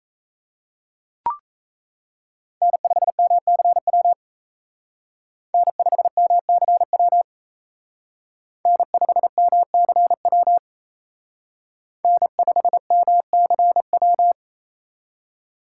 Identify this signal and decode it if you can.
Morse code, heard as a keyed tone, about 36 wpm, 700 Hz. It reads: N5MCW N5MCW N5MCW N5MCW